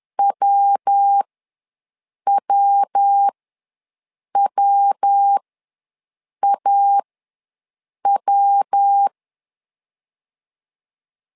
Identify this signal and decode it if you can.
Morse code, heard as a keyed tone, about 10 wpm, spaced as for 6 wpm, 800 Hz. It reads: WWWAW
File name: LEZIONI / 3 MWRO / 3 MW ws 10wpm 2213